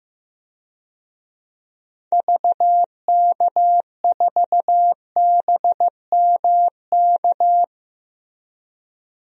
Morse code, heard as a keyed tone, about 15 wpm, 700 Hz. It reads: VK4BMK
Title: DXCC-100 Call Signs - 6 Characters - 15wpm